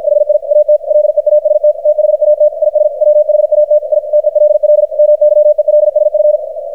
Morse code, heard as a keyed tone, about 52 wpm, 600 Hz, with radio noise. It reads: HI AT LEAST THAT IS WHAT I TELL MYELSF
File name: archived / CRSnet